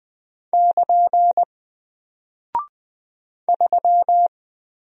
Morse code, heard as a keyed tone, about 20 wpm, 700 Hz. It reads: <KN> 3
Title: Single Character (Letter:Number:Punctuation:Prosign) Alt 20wpm